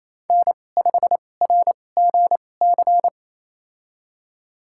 Morse code, heard as a keyed tone, about 28 wpm, spaced as for 20 wpm, 700 Hz. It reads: N5RGC
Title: USA Call Signs - 5 Characters - 2x Character Spacing - 28wpm